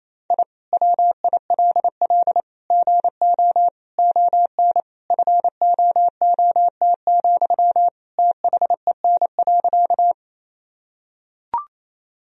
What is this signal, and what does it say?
Morse code, heard as a keyed tone, about 28 wpm, 700 Hz. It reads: I WILL GO ON FOOT, THEN.
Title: Sentences from Top 1000 Words - Review Part 3 - 28wpm